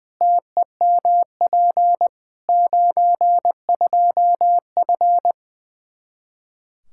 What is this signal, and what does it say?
Morse code, heard as a keyed tone, about 20 wpm, 700 Hz. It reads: TEMP 92F
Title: Practice QSO - Exchange 2 - Info - 20wpm